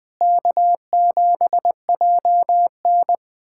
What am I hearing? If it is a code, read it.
Morse code, heard as a keyed tone, about 20 wpm, 700 Hz. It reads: K7JN
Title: USA Callsign DE Callsign - 4 Characters - 20wpm